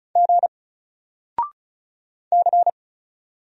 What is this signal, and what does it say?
Morse code, heard as a keyed tone, about 35 wpm, 700 Hz. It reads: G C
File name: Single-Letter-Number-35wpm